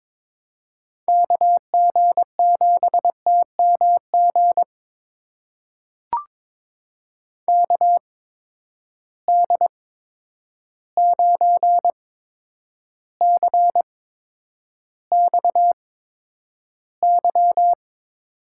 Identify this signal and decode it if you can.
Morse code, heard as a keyed tone, about 23 wpm, spaced as for 10 wpm, 700 Hz. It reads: KG7TMG KD9CXY
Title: USA Call Signs - 8x Character Spacing - 22wpm